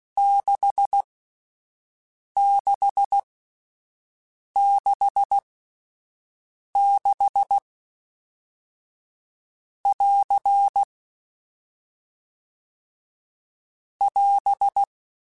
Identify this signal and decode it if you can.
Morse code, heard as a keyed tone, about 16 wpm, spaced as for 5 wpm, 800 Hz. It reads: 6666<AR> <AS>